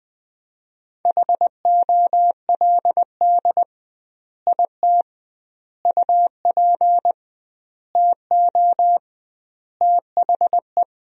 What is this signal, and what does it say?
Morse code, heard as a keyed tone, about 20 wpm, 700 Hz. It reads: HOLD IT UP TO THE
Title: 2x Word-Spacing - Sentences from Top 400 Words - 20wpm